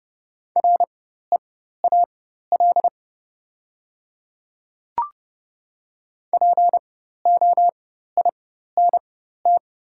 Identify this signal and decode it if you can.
Morse code, heard as a keyed tone, about 30 wpm, spaced as for 14 wpm, 690 Hz. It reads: REAL POINT
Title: 4x Character Spacing - Top 500 Words - 30wpm